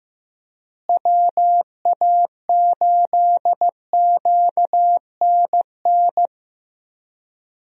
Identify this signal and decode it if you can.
Morse code, heard as a keyed tone, about 16 wpm, 700 Hz. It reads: WA8QNN
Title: USA Call Signs - 6 Characters - Repeated 3x - 15wpm